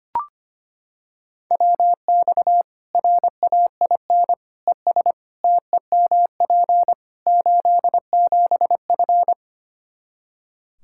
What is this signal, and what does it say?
Morse code, heard as a keyed tone, about 25 wpm, 700 Hz. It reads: WX RAIN ES TEMP 87F
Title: Practice QSO - Exchange 2 - Info - 25wpm